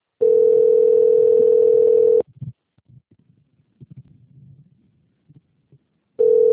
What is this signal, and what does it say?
Ringback tone (North America), 440 + 480 Hz.